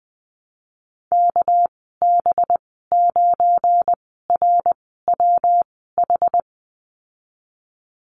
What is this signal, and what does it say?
Morse code, heard as a keyed tone, about 20 wpm, spaced as for 15 wpm, 700 Hz. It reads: KB9RWH
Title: 231 Lesson 37g - Letter x - Call Signs 20wpm